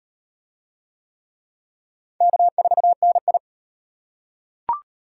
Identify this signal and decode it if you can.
Morse code, heard as a keyed tone, about 37 wpm, 700 Hz. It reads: K4NI